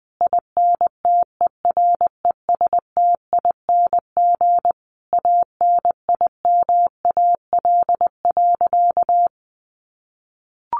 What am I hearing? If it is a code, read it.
Morse code, heard as a keyed tone, about 20 wpm, 700 Hz. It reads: INTERESTING ANIMAL.
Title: Easy Phrases 20wpm